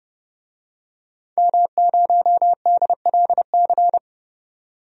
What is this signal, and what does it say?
Morse code, heard as a keyed tone, about 29 wpm, 700 Hz. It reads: M0DLC
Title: DXCC-100 Call Signs - 5 Characters - 30wpm